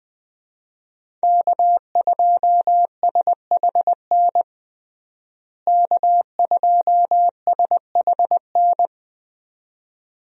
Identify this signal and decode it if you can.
Morse code, heard as a keyed tone, about 21 wpm, 700 Hz. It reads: K2SHN K2SHN
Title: USA Call Signs - 5 Characters - Repeated 2x - 20wpm